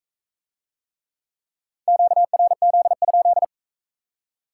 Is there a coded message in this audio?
Yes: Morse code from 1.9 to 3.4 s, heard as a keyed tone, about 39 wpm, 700 Hz. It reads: QRZ?